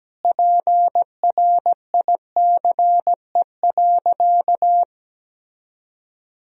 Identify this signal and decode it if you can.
Morse code, heard as a keyed tone, about 17 wpm, 700 Hz. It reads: PRICE.
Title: Sentences from Top 1000 Words - Review Part 4 - 17wpm